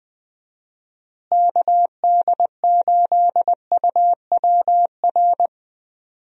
Morse code, heard as a keyed tone, about 20 wpm, 700 Hz. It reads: KD8UWR
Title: USA Call Signs - 6 Characters - Repeated 2x - 20wpm